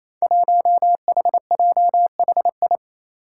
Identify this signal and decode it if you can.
Morse code, heard as a keyed tone, about 28 wpm, 700 Hz. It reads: 1HJHI
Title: Five Challenging Letter-Number Combinations 28wpm